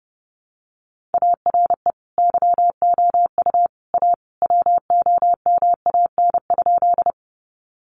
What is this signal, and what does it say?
Morse code, heard as a keyed tone, about 30 wpm, 700 Hz. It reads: ARE YOU A WOMAN?
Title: Sentences from Top 1100 Words 30wpm